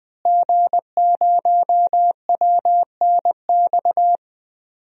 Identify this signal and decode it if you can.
Morse code, heard as a keyed tone, about 21 wpm, 700 Hz. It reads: G0WNX
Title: DXCC-100 Call Signs - 5 Characters - Repeated 3x - 20wpm